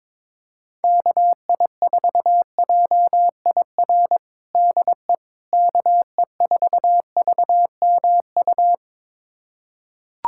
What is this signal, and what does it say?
Morse code, heard as a keyed tone, about 22 wpm, 700 Hz. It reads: KI4JIR DE KE4VMU
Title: USA Callsign DE Callsign - 6 Characters - 22wpm